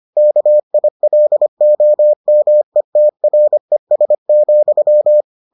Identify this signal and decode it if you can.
Morse code, heard as a keyed tone, about 25 wpm, 600 Hz. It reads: KILOMETRES,